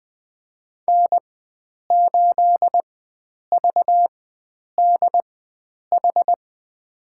Morse code, heard as a keyed tone, about 20 wpm, spaced as for 9 wpm, 700 Hz. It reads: N8VDH
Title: USA Call Signs - 5 Characters - 4x Character Spacing - 20wpm